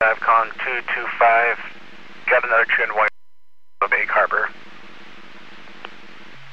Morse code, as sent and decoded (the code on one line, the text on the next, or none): none